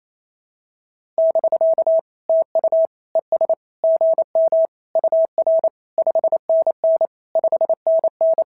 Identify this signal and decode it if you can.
Morse code, heard as a keyed tone, about 25 wpm, 650 Hz. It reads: <BK> TU ES GM UR 5NN 5NN
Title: SOTA QSO Practice 28wpm